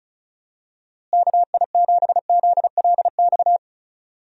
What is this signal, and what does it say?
Morse code, heard as a keyed tone, about 36 wpm, 700 Hz. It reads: KI7ZLX
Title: USA Call Signs - 6 Characters - Repeated 2x - 35wpm